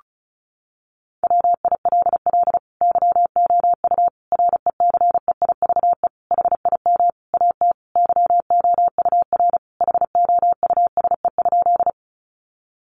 Morse code, heard as a keyed tone, about 34 wpm, 700 Hz. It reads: WILL YOU RECEIVE HIM AT YOUR HOUSE?
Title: Sentences from Top 1000 Words 35wpm